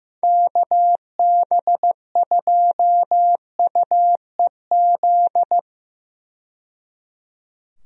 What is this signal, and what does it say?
Morse code, heard as a keyed tone, about 15 wpm, 700 Hz. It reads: KB2UEZ
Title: USA Call Signs - 6 Characters - 15wpm